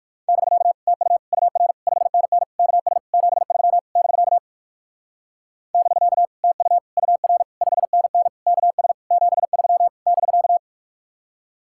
Morse code, heard as a keyed tone, about 65 wpm, spaced as for 57 wpm, 700 Hz. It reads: <BK> TU UR 5NN KS 73 <BK> <BK> TU UR 5NN KS 73 <BK>